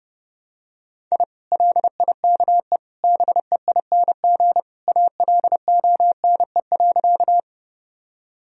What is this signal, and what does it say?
Morse code, heard as a keyed tone, about 32 wpm, 700 Hz. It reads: I LIKE BEING ALONE.
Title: Sentences from Top 1000 Words - Review Part 3 - 30wpm